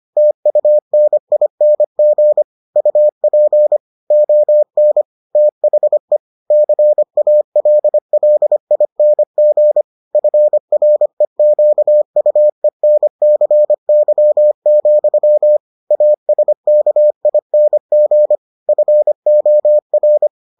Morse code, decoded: TUNING UP ON THE CALLING FREQUENCY, ASKING FOR

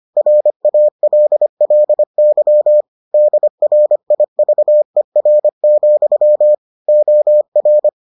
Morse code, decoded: RALLY DRIVER, OR

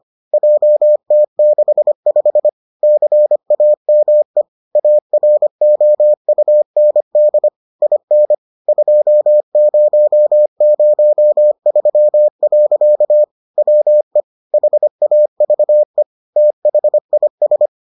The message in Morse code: JT65 CAME AROUND IN 2003. WE HAVE THIS

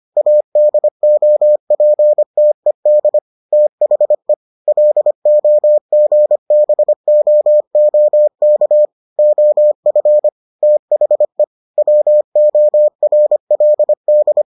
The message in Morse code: ADOPTED THE LOGBOOK OF THE WORLD

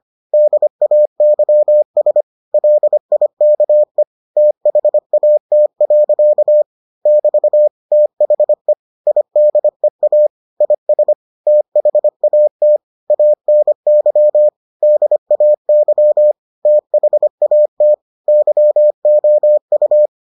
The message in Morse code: DAYS LIKE THAT. <BT> THE IDEA IS THAT ANY DAY THAT YOU

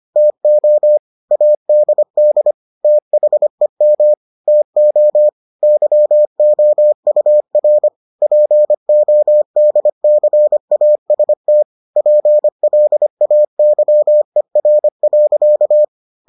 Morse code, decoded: TO ADD THEM TO YOUR PODCAST PLAYER.

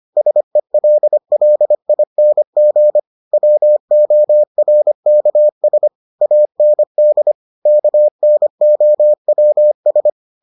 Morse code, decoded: SELLING WORKS AND KNOWS

